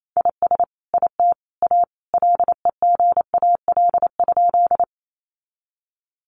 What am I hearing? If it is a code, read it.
Morse code, heard as a keyed tone, about 28 wpm, 700 Hz. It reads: IS IT A LEGAL?
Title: Sentences from Top 1200 Words 28wpm